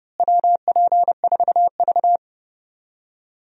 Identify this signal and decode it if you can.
Morse code, heard as a keyed tone, about 32 wpm, 700 Hz. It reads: WP4V
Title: USA Call Signs - 4 Characters - Repeated 3x - 30wpm